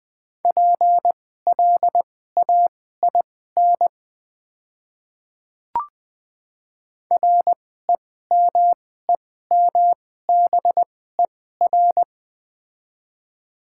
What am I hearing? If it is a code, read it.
Morse code, heard as a keyed tone, about 20 wpm, spaced as for 14 wpm, 700 Hz. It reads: PLAIN REMEMBER
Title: 2x Character Spacing - Top 500 Words - 20wpm